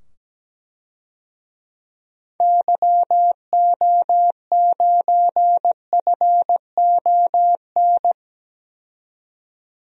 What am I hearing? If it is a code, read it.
Morse code, heard as a keyed tone, about 17 wpm, 700 Hz. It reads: YO9FON
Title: DXCC-100 Call Signs - 6 Characters - 17wpm